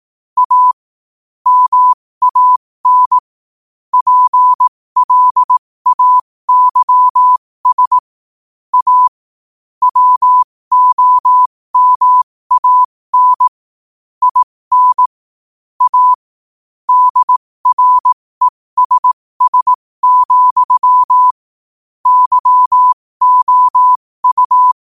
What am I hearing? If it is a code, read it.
Morse code, heard as a keyed tone, about 18 wpm, spaced as for 15 wpm, 1000 Hz. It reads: A MAN PLAYS A WOMAN IN A DRESS, YOU
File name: qotdfiles / 15wpm